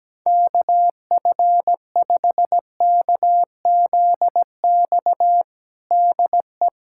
Morse code, heard as a keyed tone, about 17 wpm, 700 Hz. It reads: KF5KZX DE
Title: USA Callsign DE Callsign 17wpm